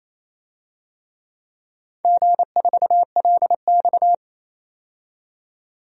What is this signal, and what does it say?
Morse code, heard as a keyed tone, about 28 wpm, 700 Hz. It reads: G4LX
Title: DXCC-100 Call Signs - 4 Characters - 28wpm